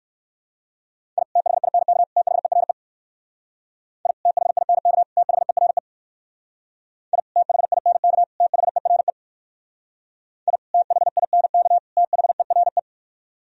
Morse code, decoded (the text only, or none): I THINK THERE I THINK THERE I THINK THERE I THINK THERE